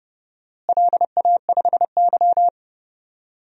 Morse code, heard as a keyed tone, about 30 wpm, 700 Hz. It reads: LA5Y
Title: DXCC-100 Call Signs - 4 Characters - Repeated 3x - 30wpm